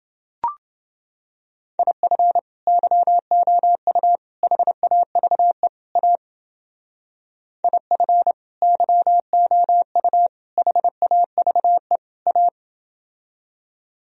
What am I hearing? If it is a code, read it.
Morse code, heard as a keyed tone, about 29 wpm, 700 Hz. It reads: IF YOU HAVE A IF YOU HAVE A